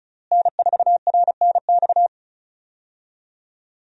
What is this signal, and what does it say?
Morse code, heard as a keyed tone, about 34 wpm, 700 Hz. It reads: N4RNX
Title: USA Call Signs - 5 Characters - Repeated 2x - 35wpm